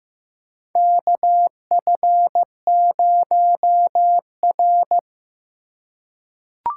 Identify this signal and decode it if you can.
Morse code, heard as a keyed tone, about 15 wpm, 700 Hz. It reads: KF0R
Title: USA Call Signs - 4 Characters - 3x Character Spacing - 15wpm